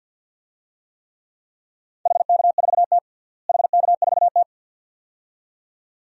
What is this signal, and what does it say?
Morse code, heard as a keyed tone, about 53 wpm, 700 Hz. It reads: HK4T HK4T
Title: DXCC-100 Call Signs - 4 Characters - Repeated 2x - 50wpm